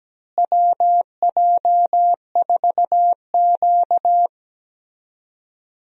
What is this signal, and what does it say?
Morse code, heard as a keyed tone, about 17 wpm, 700 Hz. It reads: WJ4Q
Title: USA Call Signs - 4 Characters - Repeated 3x - 17wpm